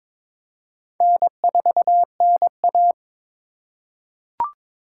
Morse code, heard as a keyed tone, about 22 wpm, 700 Hz. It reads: N4NA
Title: USA Call Signs - 3x Character Spacing - 22wpm